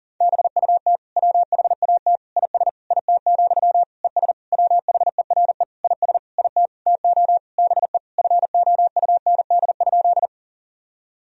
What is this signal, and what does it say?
Morse code, heard as a keyed tone, about 40 wpm, 700 Hz. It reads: BUT WHAT IS IT, ES WHERE IS IT TO BE FOUND?
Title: Sentences from Top 300 Words and Abbreviations 40wpm